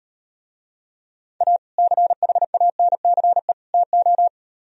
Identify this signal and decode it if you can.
Morse code, heard as a keyed tone, about 38 wpm, 700 Hz. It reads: A CHANCE TO